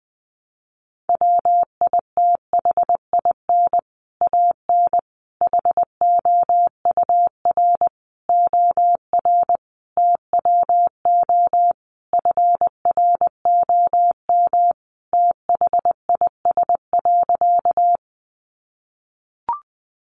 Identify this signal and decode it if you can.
Morse code, heard as a keyed tone, about 20 wpm, 700 Hz. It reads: WITHIN AN HOUR OR TWO FROM THIS.
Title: Sentences from Top 700 Words 20wpm